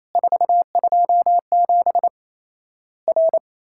Morse code, heard as a keyed tone, about 28 wpm, 700 Hz. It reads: 427 R
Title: Worked All Europe DX Contest - QTC Exchange - 28wpm